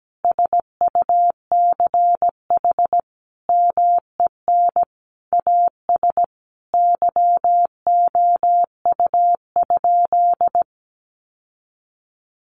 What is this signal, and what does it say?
Morse code, heard as a keyed tone, about 17 wpm, 700 Hz. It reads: SUCH MEN AS YOU?
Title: Sentences from Top 1000 Words 17wpm